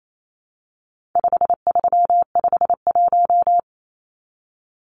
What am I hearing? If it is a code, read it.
Morse code, heard as a keyed tone, about 28 wpm, 700 Hz. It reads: 5351
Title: Numbers - 4-Digit 28wpm